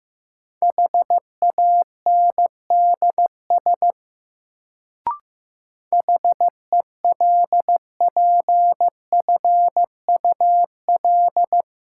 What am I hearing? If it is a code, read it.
Morse code, heard as a keyed tone, about 15 wpm, 700 Hz. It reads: HANDS HELPFUL
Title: Sentences from Top 1000 Words - Review Part 6 - New Word Drill - 15wpm